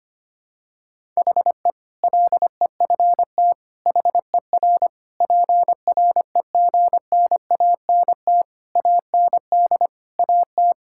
Morse code, heard as a keyed tone, about 25 wpm, 700 Hz. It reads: HE LEFT HER PREGNANT AND AT
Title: Sentences from Top 1000 Words - Review Part 6 - 25wpm